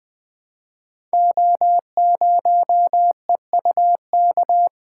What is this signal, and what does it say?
Morse code, heard as a keyed tone, about 20 wpm, 700 Hz. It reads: O0EUK
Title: Code Groups - 5 Letter:Numbers - 20wpm